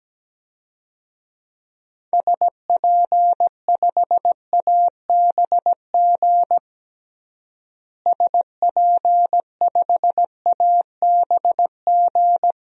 Morse code, heard as a keyed tone, about 17 wpm, 700 Hz. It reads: SP5ABG SP5ABG
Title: DXCC-100 Call Signs - 6 Characters - Repeated 2x - 17wpm